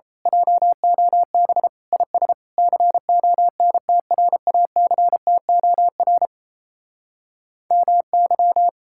Morse code, heard as a keyed tone, about 31 wpm, 700 Hz. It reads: JOB IS CONTRACTOR MY
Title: Speed-Racing Common QSO Phrases Encore - 28wpm